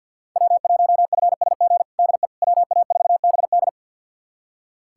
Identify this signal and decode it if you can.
Morse code, heard as a keyed tone, about 46 wpm, 690 Hz. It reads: W1FSG DE WA4BD
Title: USA Callsign DE Callsign - 5 Characters - 50wpm